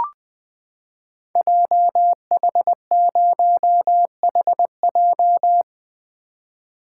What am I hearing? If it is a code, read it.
Morse code, heard as a keyed tone, about 20 wpm, 700 Hz. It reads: JH0HJ
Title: Code Groups - 5 Letter:Numbers - 20wpm